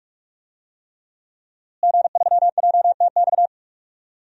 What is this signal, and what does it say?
Morse code, heard as a keyed tone, about 45 wpm, 700 Hz. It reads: G3JTX
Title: DXCC-100 Call Signs - 5 Characters - Repeated 3x - 45wpm